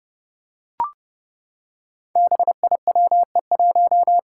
Morse code, heard as a keyed tone, about 28 wpm, 700 Hz. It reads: BIWE1
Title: Five Challenging Letter-Number Combinations 30wpm